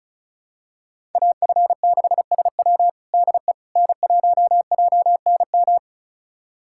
Morse code, heard as a keyed tone, about 36 wpm, 700 Hz. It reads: AF6SW DE N1JNM
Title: USA Callsign DE Callsign - 5 Characters - 35wpm